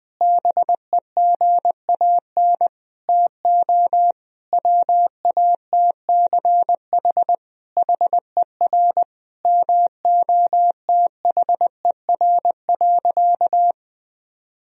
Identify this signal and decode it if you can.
Morse code, heard as a keyed tone, about 21 wpm, 700 Hz. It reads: BEGAN TO WATCH HER MOTHER.
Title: Sentences from Top 1000 Words - Review Part 1 - 20wpm